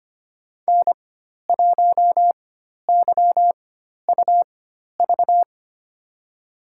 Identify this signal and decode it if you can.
Morse code, heard as a keyed tone, about 25 wpm, spaced as for 12 wpm, 700 Hz. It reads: N1YUV
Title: USA Call Signs - 4x Character Spacing - 25wpm